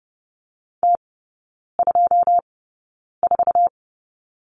Morse code, heard as a keyed tone, about 30 wpm, spaced as for 9 wpm, 700 Hz. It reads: T24
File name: Lesson 19e - Letter w - Three Characters 30wpm